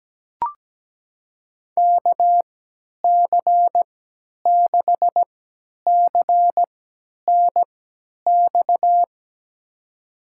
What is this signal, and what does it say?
Morse code, heard as a keyed tone, about 17 wpm, spaced as for 10 wpm, 700 Hz. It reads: KC6CNX
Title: USA Call Signs - 6 Characters - 3x Character Spacing - 17wpm